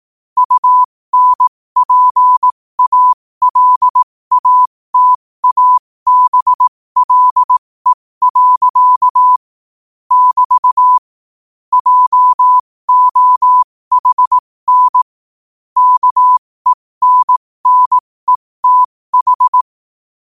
Morse code, heard as a keyed tone, about 18 wpm, spaced as for 15 wpm, 1000 Hz. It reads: UNPALATABLE. <BT> JOHN KENNETH
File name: qotdfiles / 15wpm